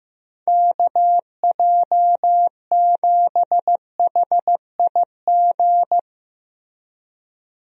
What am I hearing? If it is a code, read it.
Morse code, heard as a keyed tone, about 15 wpm, 700 Hz. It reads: KJ7HIG
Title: USA Call Signs - 6 Characters - 15wpm